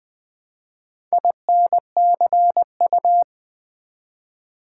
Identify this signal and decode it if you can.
Morse code, heard as a keyed tone, about 20 wpm, 700 Hz. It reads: INCU
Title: Build A Word - 1 Letter At A Time - 5-Letter Words - 20wpm